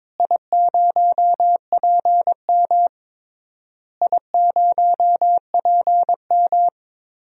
Morse code, heard as a keyed tone, about 22 wpm, 700 Hz. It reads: I0PM I0PM
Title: DXCC-100 Call Signs - 4 Characters - Repeated 3x - 22wpm